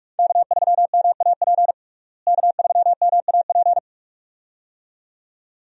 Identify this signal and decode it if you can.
Morse code, heard as a keyed tone, about 42 wpm, 700 Hz. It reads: K3MAP K3MAP
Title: USA Call Signs - 5 Characters - Repeated 2x - 45wpm